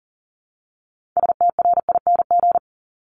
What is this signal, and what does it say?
Morse code, heard as a keyed tone, about 38 wpm, 700 Hz. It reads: STRING
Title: Multi-letter Phonemes Repeated 3x 40wpm